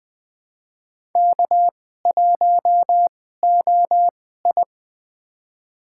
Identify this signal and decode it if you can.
Morse code, heard as a keyed tone, about 21 wpm, spaced as for 15 wpm, 700 Hz. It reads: K1OI
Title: USA Call Signs - 2x Character Spacing - 20wpm